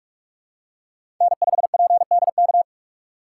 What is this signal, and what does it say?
Morse code, heard as a keyed tone, about 44 wpm, 690 Hz. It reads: N5PDK
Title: USA Call Signs - 5 Characters - 8x Character Spacing - 45wpm